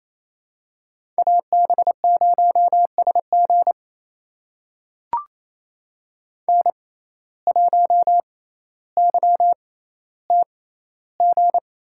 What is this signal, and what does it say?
Morse code, heard as a keyed tone, about 28 wpm, spaced as for 14 wpm, 700 Hz. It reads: AB0SG N1YTG
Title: USA Call Signs - 5 Characters - 6x Character Spacing - 28wpm